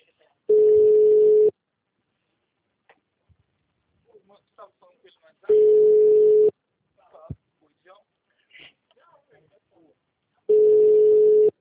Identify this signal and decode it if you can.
Ringback tone (Europe, 425 Hz), with speech.
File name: out- -2016- - - .1397